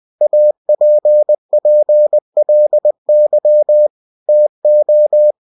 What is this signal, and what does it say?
Morse code, heard as a keyed tone, about 20 wpm, 600 Hz. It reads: APPLY TO
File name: cw-books / winnie-the-pooh / winnie-20wpm-page-0112